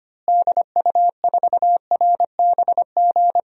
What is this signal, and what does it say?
Morse code, heard as a keyed tone, about 25 wpm, 700 Hz. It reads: DU4RBG
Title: DXCC-100 Call Signs - 6 Characters - Repeated 3x - 25wpm